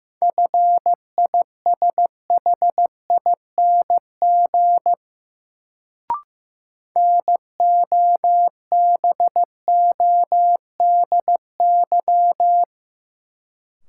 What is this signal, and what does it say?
Morse code, heard as a keyed tone, about 15 wpm, 700 Hz. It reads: FISHING NOBODY
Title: Sentences from Top 1600 Words - New Word Drill Alt - 15wpm